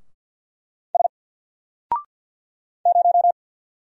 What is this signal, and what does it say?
Morse code, heard as a keyed tone, about 48 wpm, 700 Hz. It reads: S 0
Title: Single Letter-Number 50wpm